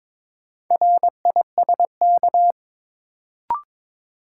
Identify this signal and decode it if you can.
Morse code, heard as a keyed tone, about 22 wpm, 700 Hz. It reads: RISK
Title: Sentences from Top 600 Words - New Word Drill - 22wpm